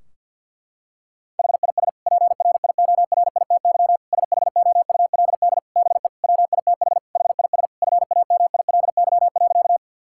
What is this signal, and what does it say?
Morse code, heard as a keyed tone, about 50 wpm, 700 Hz. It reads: HIS PRIORITY SHOULD BE WITH HIS FAMILY.
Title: Sentences from Top 1600 Words 50wpm